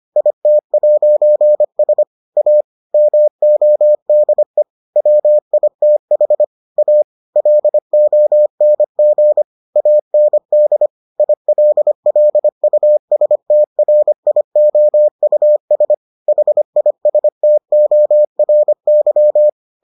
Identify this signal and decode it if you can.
Morse code, heard as a keyed tone, about 25 wpm, 600 Hz. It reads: IT'S A MODE WITH A LONG AND ILLUSTRIOUS HISTORY